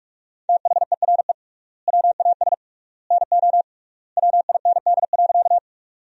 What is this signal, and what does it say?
Morse code, heard as a keyed tone, about 46 wpm, 700 Hz. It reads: THERE WAS NO WIND.
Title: 3x Word-Spacing - Sentences from Top 400 Words - 45wpm